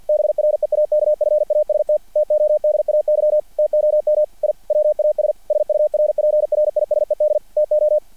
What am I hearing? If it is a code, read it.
Morse code, heard as a keyed tone, about 50 wpm, 600 Hz. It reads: BREAKFAST TODAY TOM I WAS SURPRISED TO
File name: archived / CRSnet